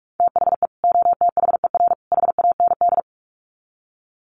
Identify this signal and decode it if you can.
Morse code, heard as a keyed tone, about 44 wpm, 700 Hz. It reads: THE OTHER HAND
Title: Sets of 3 Words 45wpm